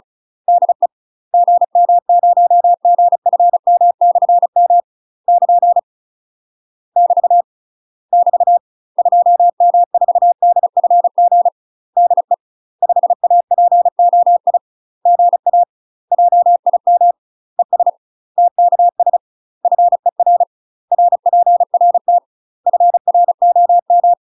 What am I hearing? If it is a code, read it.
Morse code, heard as a keyed tone, about 34 wpm, 700 Hz. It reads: DE GM0GFM/M <KN> <BT> <BT> 2M4DFG DE 5APOI GA JIM ES TKS FER RPRT FROM
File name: UK MASTER QSOs TRAINING / 35WPM / UK MASTER QSOs 0028